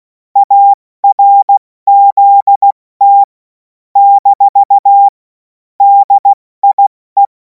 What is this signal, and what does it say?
Morse code, heard as a keyed tone, about 16 wpm, 800 Hz. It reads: ARZT - DIE
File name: Morsekurs / Witze / Witz 24. Februar 2026 70 F80